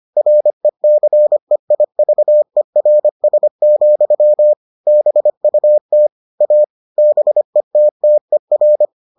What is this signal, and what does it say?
Morse code, heard as a keyed tone, about 25 wpm, 600 Hz. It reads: RECEIVERS, BUT A BETTER